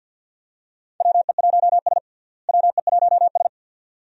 Morse code, heard as a keyed tone, about 49 wpm, 700 Hz. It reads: WE1S WE1S
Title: USA Call Signs - 4 Characters - Repeated 2x - 50wpm